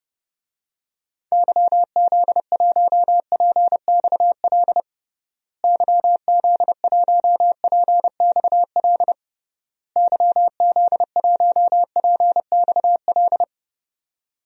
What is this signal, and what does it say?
Morse code, heard as a keyed tone, about 30 wpm, 700 Hz. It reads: YZ1PXL YZ1PXL YZ1PXL
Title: DXCC-100 Call Signs - 6 Characters - Repeated 3x - 30wpm